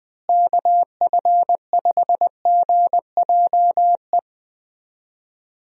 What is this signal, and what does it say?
Morse code, heard as a keyed tone, about 20 wpm, 700 Hz. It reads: KF5GJE
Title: USA Call Signs - 6 Characters - Repeated 3x - 20wpm